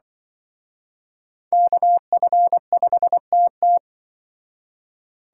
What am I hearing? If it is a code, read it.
Morse code, heard as a keyed tone, about 24 wpm, 700 Hz. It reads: KF5TT
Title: Speed Racing - USA 2x2 Call Signs 20wpm